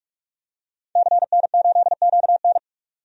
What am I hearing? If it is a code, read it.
Morse code, heard as a keyed tone, about 44 wpm, 690 Hz. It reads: CN8QN
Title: DXCC-100 Call Signs - 5 Characters - 45wpm